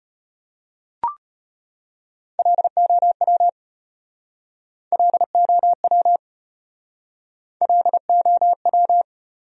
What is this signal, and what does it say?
Morse code, heard as a keyed tone, about 34 wpm, 700 Hz. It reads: LOW LOW LOW